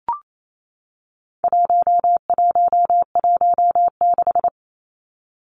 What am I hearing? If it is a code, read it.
Morse code, heard as a keyed tone, about 28 wpm, 700 Hz. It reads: 1116
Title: Numbers - 4-Digit 28wpm